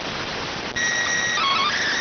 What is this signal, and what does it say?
SSTV (slow-scan television), PD 290.